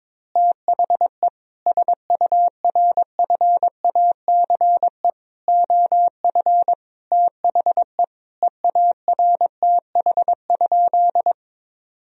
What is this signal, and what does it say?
Morse code, heard as a keyed tone, about 22 wpm, 700 Hz. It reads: THE SURFACE OF THE EARTH?
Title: Sentences from Top 1000 Words - Review Part 3 - 22wpm